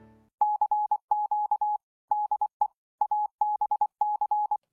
Morse code, heard as a keyed tone, about 24 wpm, 840 Hz. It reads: CQ DE ABC